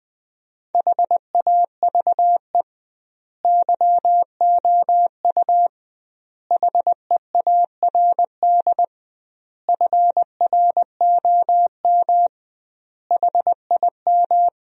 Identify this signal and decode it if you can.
Morse code, heard as a keyed tone, about 20 wpm, 700 Hz. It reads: HAVE YOU HEARD FROM HIM
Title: 2x Word-Spacing - Sentences from Top 400 Words - 20wpm